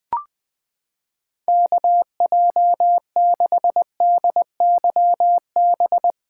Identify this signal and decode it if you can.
Morse code, heard as a keyed tone, about 20 wpm, 700 Hz. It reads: KJ6DYB
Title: USA Callsign DE Callsign 20wpm